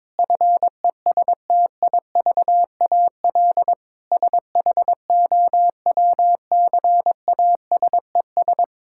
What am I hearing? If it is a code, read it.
Morse code, heard as a keyed tone, about 22 wpm, 700 Hz. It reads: FESTIVAL SHOWCASES
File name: Difficult Sentences with Hints - 22wpm